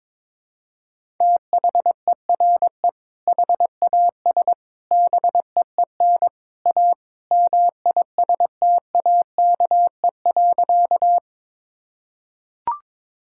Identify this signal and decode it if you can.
Morse code, heard as a keyed tone, about 22 wpm, 700 Hz. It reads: THERE HAS BEEN A MISTAKE.
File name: practice / 22wpm / Sentences with 3-Word Phrases - 22wpm